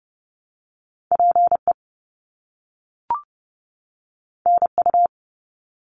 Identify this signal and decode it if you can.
Morse code, heard as a keyed tone, about 30 wpm, 700 Hz. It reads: PE NU
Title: Canadian Provinces and Territories 30wpm